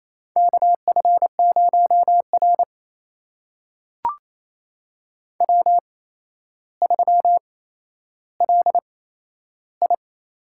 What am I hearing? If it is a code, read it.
Morse code, heard as a keyed tone, about 29 wpm, spaced as for 12 wpm, 700 Hz. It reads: KF0R W3LI
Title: USA Call Signs - 4 Characters - 8x Character Spacing - 28wpm